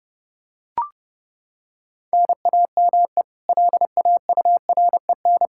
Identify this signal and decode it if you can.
Morse code, heard as a keyed tone, about 30 wpm, 700 Hz. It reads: NAME LAUREN
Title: Practice QSO - Exchange 1 - Beginning - 30wpm